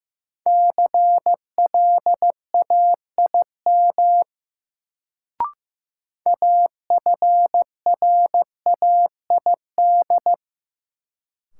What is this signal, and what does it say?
Morse code, heard as a keyed tone, about 15 wpm, 700 Hz. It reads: CLAIM AFRAID
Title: Sentences from Top 1500 Words - New Word Drill Alt - 15wpm